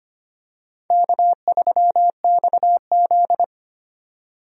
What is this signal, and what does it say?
Morse code, heard as a keyed tone, about 25 wpm, 700 Hz. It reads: K3XZ
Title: USA Call Signs - 4 Characters - 25wpm